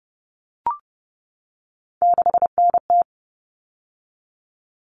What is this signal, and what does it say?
Morse code, heard as a keyed tone, about 30 wpm, 700 Hz. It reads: 6NT
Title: All Cut Numbers - 3-Digit 30wpm